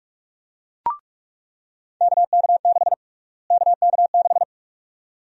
Morse code, heard as a keyed tone, about 46 wpm, 690 Hz. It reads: KK6 KK6
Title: USA Call Sign Prefixes - Repeated 2x - 45wpm